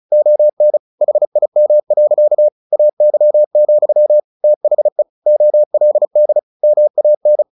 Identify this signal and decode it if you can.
Morse code, heard as a keyed tone, about 34 wpm, 600 Hz. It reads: ON HIM. AY, THE OLD MAN
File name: the old man and the sea 2 35wpm